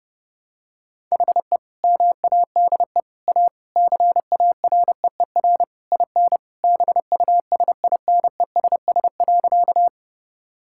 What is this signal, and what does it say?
Morse code, heard as a keyed tone, about 30 wpm, 700 Hz. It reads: HE MADE A CAREER IN BUSINESS.
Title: Sentences from Top 1000 Words - Review Part 6 - 30wpm